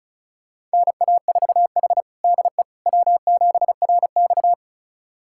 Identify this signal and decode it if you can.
Morse code, heard as a keyed tone, about 36 wpm, 700 Hz. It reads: NA4H DE W7RX
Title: USA Callsign DE Callsign - 4 Characters - 35wpm